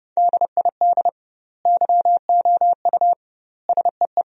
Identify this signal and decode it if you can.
Morse code, heard as a keyed tone, about 30 wpm, 700 Hz. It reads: DID YOU SEE
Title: 2x Word-Spacing - Sentences from Top 100 Words - 30wpm